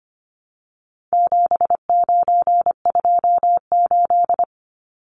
Morse code, heard as a keyed tone, about 25 wpm, 700 Hz. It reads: 7928
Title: Numbers - 4-Digit 25wpm